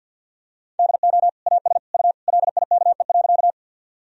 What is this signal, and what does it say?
Morse code, heard as a keyed tone, about 51 wpm, 700 Hz. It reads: DO AS U LIKE.
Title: Sentences from Top 100 Words and Abbreviations 50wpm